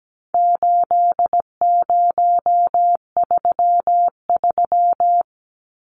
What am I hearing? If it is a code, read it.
Morse code, heard as a keyed tone, about 17 wpm, 700 Hz. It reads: 8033
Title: Numbers - 4-Digit 17wpm